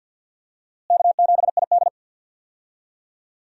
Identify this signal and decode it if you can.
Morse code, heard as a keyed tone, about 49 wpm, 700 Hz. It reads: K7ID
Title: USA Call Signs - 4 Characters - Repeated 2x - 50wpm